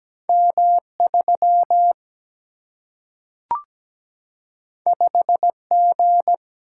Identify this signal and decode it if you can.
Morse code, heard as a keyed tone, about 17 wpm, 700 Hz. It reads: M 3 5 G
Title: Single-Letter Single-Number 17wpm